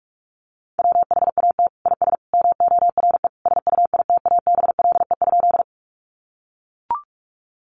Morse code, heard as a keyed tone, about 44 wpm, 700 Hz. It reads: WHAT IS MORE SUITABLE?
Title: Sentences from Top 1400 Words 45wpm